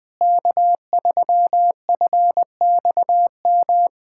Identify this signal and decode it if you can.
Morse code, heard as a keyed tone, about 21 wpm, 700 Hz. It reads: K3FXM
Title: USA Callsign DE Callsign - 5 Characters - 20wpm